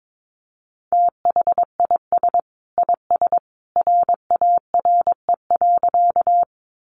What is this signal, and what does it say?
Morse code, heard as a keyed tone, about 22 wpm, 700 Hz. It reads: THIS IS RARE.
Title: Sentences from Top 1100 Words 22wpm